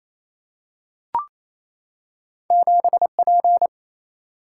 Morse code, heard as a keyed tone, about 28 wpm, 700 Hz. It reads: 7 P
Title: Single-Letter Single-Number 28wpm